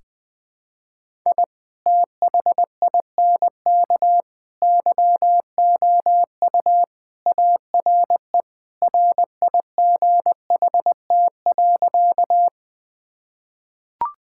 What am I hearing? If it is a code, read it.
Morse code, heard as a keyed tone, about 21 wpm, 700 Hz. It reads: I THINK YOU ARE RIGHT.
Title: Sentences with 3-Word Phrases - 20wpm